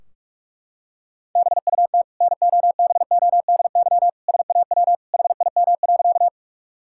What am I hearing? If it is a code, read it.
Morse code, heard as a keyed tone, about 44 wpm, 700 Hz. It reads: BUT NOBODY SAW HIM.
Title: Sentences from Top 1600 Words 45wpm